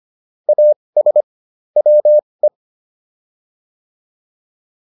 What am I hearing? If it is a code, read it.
Morse code, heard as a keyed tone, about 25 wpm, spaced as for 20 wpm, 600 Hz. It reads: AS WE